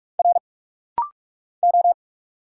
Morse code, heard as a keyed tone, about 46 wpm, 690 Hz. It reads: R O
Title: Single Letters - Rapid-Fire 45wpm